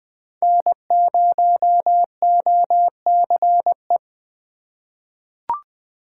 Morse code, heard as a keyed tone, about 20 wpm, 700 Hz. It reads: N0OCE
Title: USA Call Signs - 4x Character Spacing - 20wpm